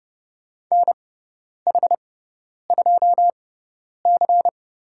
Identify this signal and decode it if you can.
Morse code, heard as a keyed tone, about 30 wpm, spaced as for 10 wpm, 700 Hz. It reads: NH2C
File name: Lesson 16g - Letter c - Call Signs 30wpm